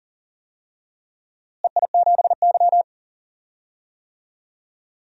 Morse code, heard as a keyed tone, about 40 wpm, 700 Hz. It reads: EI7Y
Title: DXCC-100 Call Signs - 4 Characters - Repeated 2x - 40wpm